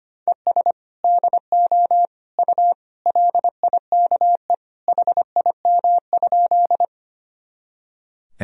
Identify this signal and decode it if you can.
Morse code, heard as a keyed tone, about 25 wpm, 700 Hz. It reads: ES DO U LIKE HIM?
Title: Sentences from Top 100 Words and Abbreviations 25wpm